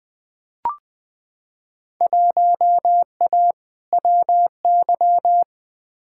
Morse code, heard as a keyed tone, about 20 wpm, 700 Hz. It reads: 1A WY
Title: Field Day Exchanges - 20wpm